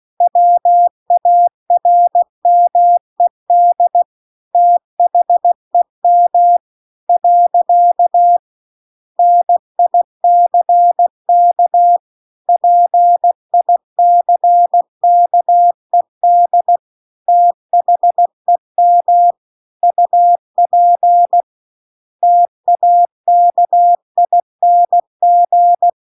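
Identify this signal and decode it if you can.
Morse code, heard as a keyed tone, about 16 wpm, 700 Hz. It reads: WARMED THEM . NICK PICKED THEM UP TAKING
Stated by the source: Morse practice text at 16 WPM.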